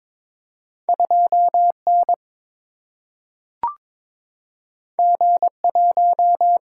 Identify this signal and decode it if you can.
Morse code, heard as a keyed tone, about 22 wpm, 700 Hz. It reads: 2N G1
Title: Single-Letter Single-Number 22wpm